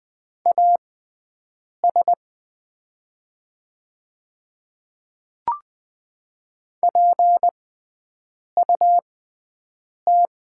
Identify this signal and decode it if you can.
Morse code, heard as a keyed tone, about 20 wpm, spaced as for 7 wpm, 700 Hz. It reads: AS PUT
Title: 6x Character Spacing - Top 200 Words - 20wpm